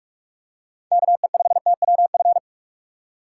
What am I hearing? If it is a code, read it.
Morse code, heard as a keyed tone, about 46 wpm, 700 Hz. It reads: KE5TWF
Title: USA Call Signs - 6 Characters - 6x Character Spacing - 45wpm